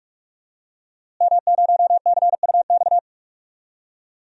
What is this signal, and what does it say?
Morse code, heard as a keyed tone, about 41 wpm, 700 Hz. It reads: M0CUX
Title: DXCC-100 Call Signs - 5 Characters - Repeated 3x - 45wpm